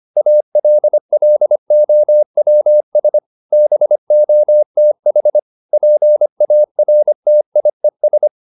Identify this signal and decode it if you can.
Morse code, heard as a keyed tone, about 25 wpm, 600 Hz. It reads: ALLOWS BOTH PARTIES